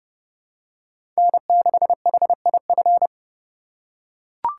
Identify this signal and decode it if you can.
Morse code, heard as a keyed tone, about 32 wpm, 700 Hz. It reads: N6HIF
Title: USA Call Signs - 5 Characters - 3x Character Spacing - 30wpm